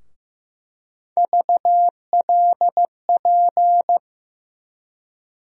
Morse code, heard as a keyed tone, about 15 wpm, 700 Hz. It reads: VLP
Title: USA Call Sign Postfixes - 3 Letters 15wpm